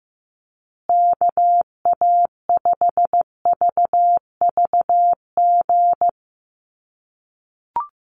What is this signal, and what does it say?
Morse code, heard as a keyed tone, about 15 wpm, 700 Hz. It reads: KA5VVG
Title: USA Call Signs Repeated 3x 15wpm